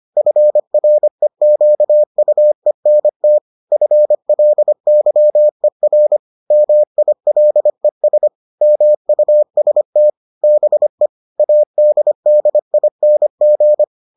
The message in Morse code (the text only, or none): FREQUENT FLYER MILES MUST BE ADDING